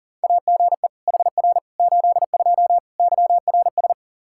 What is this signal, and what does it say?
Morse code, heard as a keyed tone, about 40 wpm, 700 Hz. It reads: AGE HR 82 YRS
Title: K1USN SST QSO Practice 40wpm